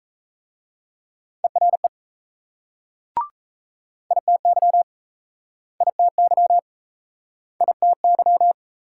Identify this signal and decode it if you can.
Morse code, heard as a keyed tone, about 35 wpm, 700 Hz. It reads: ERE ITY ITY ITY